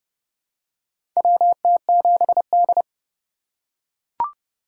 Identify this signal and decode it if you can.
Morse code, heard as a keyed tone, about 30 wpm, 700 Hz. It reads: WT7D
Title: USA Call Signs - 4 Characters - 8x Character Spacing - 30wpm